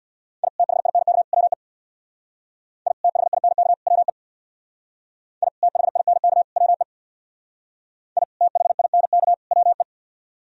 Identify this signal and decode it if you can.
Morse code, heard as a keyed tone, about 64 wpm, 700 Hz. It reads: I THINK WE I THINK WE I THINK WE I THINK WE